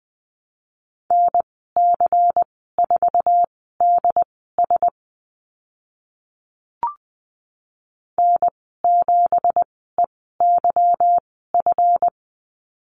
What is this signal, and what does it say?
Morse code, heard as a keyed tone, about 20 wpm, spaced as for 15 wpm, 700 Hz. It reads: NC4DS N7EYF
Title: 186 Lesson 30g - Letter b - Call Signs 20wpm